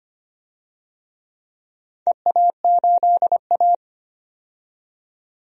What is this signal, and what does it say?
Morse code, heard as a keyed tone, about 25 wpm, 700 Hz. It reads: EA8A
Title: DXCC-100 Call Signs - 4 Characters - 25wpm